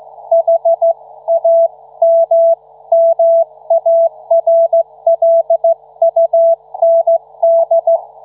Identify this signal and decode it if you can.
Morse code, heard as a keyed tone, about 14 wpm, 670 Hz. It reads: HAMMARLUND